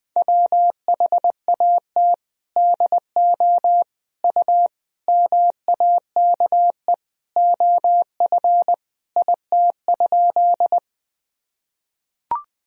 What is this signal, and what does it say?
Morse code, heard as a keyed tone, about 20 wpm, 700 Hz. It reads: WHAT DO U MAKE OF IT?
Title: Sentences from Top 100 Words and Abbreviations 20wpm